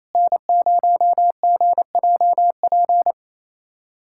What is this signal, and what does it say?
Morse code, heard as a keyed tone, about 28 wpm, 700 Hz. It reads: N0GJP
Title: SOTA QSO Practice 28wpm